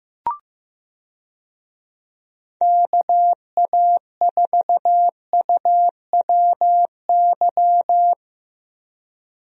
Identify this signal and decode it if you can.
Morse code, heard as a keyed tone, about 15 wpm, 700 Hz. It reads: KA4UWY
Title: USA Call Signs - 6 Characters - Repeated 3x - 15wpm